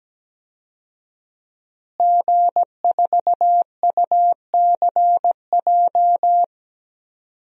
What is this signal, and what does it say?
Morse code, heard as a keyed tone, about 17 wpm, 700 Hz. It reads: G4UCJ
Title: DXCC-100 Call Signs - 5 Characters - Repeated 2x - 17wpm